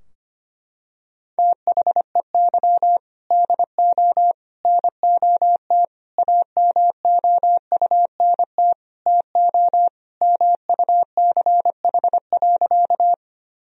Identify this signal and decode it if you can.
Morse code, heard as a keyed tone, about 25 wpm, 700 Hz. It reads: THEY DO NOT AMOUNT TO MUCH.
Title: Sentences from Top 1000 Words - Review Part 4 - 25wpm